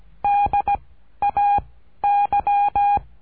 Morse code, heard as a keyed tone, about 17 wpm, 810 Hz. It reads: D A Y
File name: cursos / CURSO DE TELEGRAFIA / Telegrafia - Trilha 13